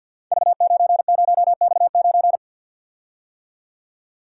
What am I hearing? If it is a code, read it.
Morse code, heard as a keyed tone, about 51 wpm, 700 Hz. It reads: V90X9
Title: Code Groups - 5 Letter:Numbers - 50wpm